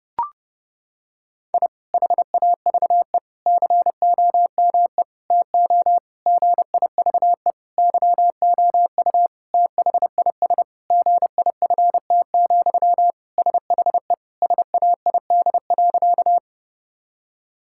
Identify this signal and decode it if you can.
Morse code, heard as a keyed tone, about 30 wpm, 700 Hz. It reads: I HAVE COME TO GIVE YOU THIS GIFT, SHE SAID.
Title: Sentences from Top 1000 Words - Review Part 6 - 30wpm